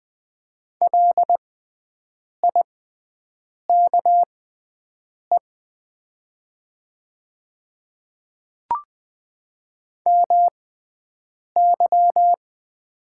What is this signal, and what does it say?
Morse code, heard as a keyed tone, about 20 wpm, spaced as for 7 wpm, 700 Hz. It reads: LIKE MY
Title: 6x Character Spacing - Top 100 Words - 20wpm